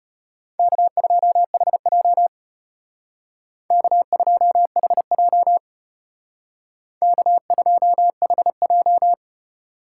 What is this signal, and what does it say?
Morse code, heard as a keyed tone, about 32 wpm, 700 Hz. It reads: K2HJ K2HJ K2HJ